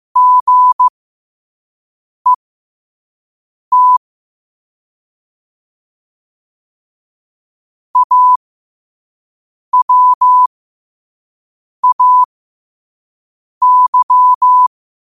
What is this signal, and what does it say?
Morse code, heard as a keyed tone, about 15 wpm, spaced as for 5 wpm, 1000 Hz. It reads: GET AWAY
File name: qotdfiles / 5wpm